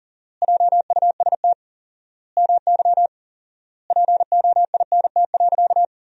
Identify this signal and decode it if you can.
Morse code, heard as a keyed tone, about 37 wpm, 700 Hz. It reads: JUST MY POINT.
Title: 4x Word-Spacing - Sentences from Top 200 Words - 40wpm